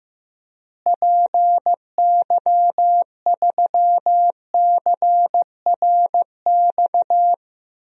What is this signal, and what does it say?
Morse code, heard as a keyed tone, about 15 wpm, 700 Hz. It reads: PY3CRX
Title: DXCC-100 Call Signs - 6 Characters - Repeated 3x - 15wpm